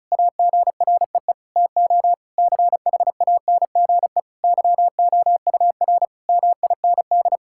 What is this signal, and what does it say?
Morse code, heard as a keyed tone, about 36 wpm, 700 Hz. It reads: AGREE TO CHANGE YOUR MIND
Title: Sentences from Top 1000 Words - Review Part 6 - 35wpm